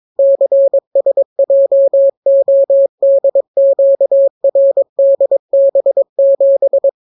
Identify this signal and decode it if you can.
Morse code, heard as a keyed tone, about 22 wpm, 550 Hz. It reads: CSJODQRDB7